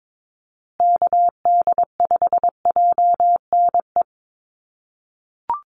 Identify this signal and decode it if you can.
Morse code, heard as a keyed tone, about 22 wpm, 700 Hz. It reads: KD5JNE
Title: USA Call Signs Repeated 3x 22wpm